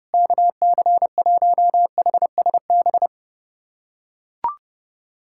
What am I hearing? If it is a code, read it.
Morse code, heard as a keyed tone, about 30 wpm, 690 Hz. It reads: KC1HSB
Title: USA Callsign DE Callsign - 6 Characters - 30wpm